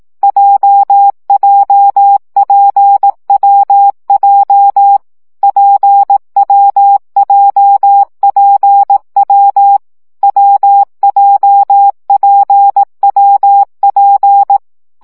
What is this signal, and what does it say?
Morse code, heard as a keyed tone, about 18 wpm, 800 Hz. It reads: JJPWJ PWJPW WJPWP
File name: CW gyakorlas w p j 15 baud